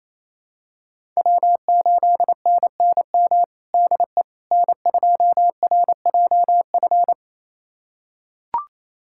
Morse code, heard as a keyed tone, about 28 wpm, 700 Hz. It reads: W8NNM DE N2RJF
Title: USA Callsign DE Callsign - 5 Characters - 28wpm